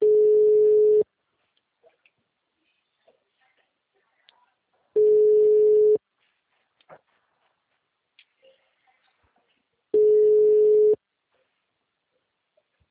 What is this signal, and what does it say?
Ringback tone (Europe, 425 Hz), with speech.